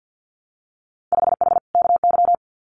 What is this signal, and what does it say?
Morse code, heard as a keyed tone, about 50 wpm, 700 Hz. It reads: 5H KY
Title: Winter Field Day Exchanges Encore 50wpm